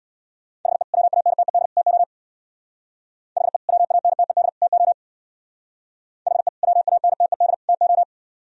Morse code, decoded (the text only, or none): HE WANTED TO HE WANTED TO HE WANTED TO